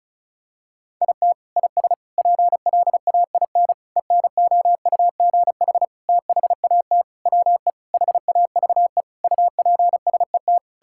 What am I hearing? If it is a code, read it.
Morse code, heard as a keyed tone, about 36 wpm, 700 Hz. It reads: IT IS PLAIN ENOUGH THAT WE HAVE UPSET
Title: Sentences from Top 1000 Words - Review Part 5 - 35wpm